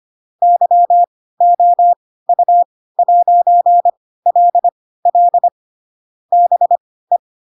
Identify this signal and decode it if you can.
Morse code, heard as a keyed tone, about 25 wpm, spaced as for 16 wpm, 700 Hz. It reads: YOU'LL BE
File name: Pinocchio / Pinocchio ch4 10 25 ik1hge